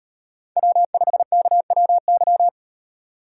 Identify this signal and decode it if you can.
Morse code, heard as a keyed tone, about 35 wpm, 700 Hz. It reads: W5KWY